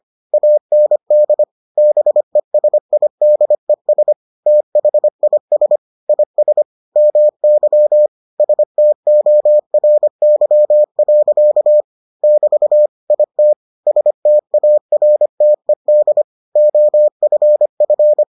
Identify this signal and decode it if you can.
Morse code, heard as a keyed tone, about 25 wpm, 600 Hz. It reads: AND BESIDES THIS IS MY STORY. <BT> IT STARTED OFF